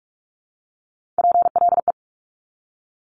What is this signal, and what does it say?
Morse code, heard as a keyed tone, about 46 wpm, 700 Hz. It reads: PLE
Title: Top Three Letter Combinations in Words 45wpm